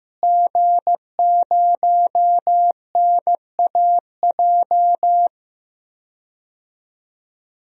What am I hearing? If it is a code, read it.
Morse code, heard as a keyed tone, about 15 wpm, 700 Hz. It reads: G0NAJ
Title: DXCC-100 Call Signs - 5 Characters - Repeated 2x - 15wpm